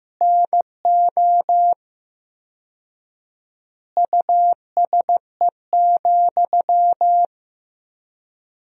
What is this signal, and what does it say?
Morse code, heard as a keyed tone, about 15 wpm, 700 Hz. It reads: NO USE,
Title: 4x Word-Spacing - Sentences from Top 500 Words - 15wpm